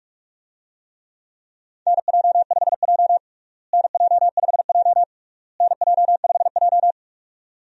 Morse code, heard as a keyed tone, about 46 wpm, 700 Hz. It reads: NJ5J NJ5J NJ5J
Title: USA Call Signs - 4 Characters - Repeated 3x - 45wpm